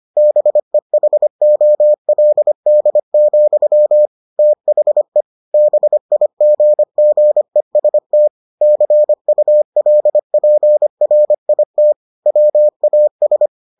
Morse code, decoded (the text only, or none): BEHOLD, THE BIGGEST CULPRIT WAS